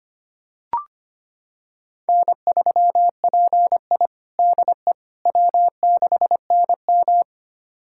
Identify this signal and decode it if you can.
Morse code, heard as a keyed tone, about 25 wpm, 700 Hz. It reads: N3PI DE W6NM
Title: USA Callsign DE Callsign - 4 Characters - 25wpm